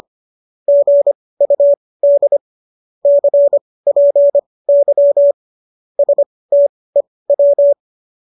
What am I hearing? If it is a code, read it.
Morse code, heard as a keyed tone, about 25 wpm, spaced as for 18 wpm, 570 Hz. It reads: GUD CPY STEW
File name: qso218 18